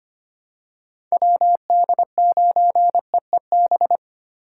Morse code, heard as a keyed tone, about 25 wpm, 700 Hz. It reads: WD9EEB
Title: USA Call Signs - 6 Characters - 4x Character Spacing - 25wpm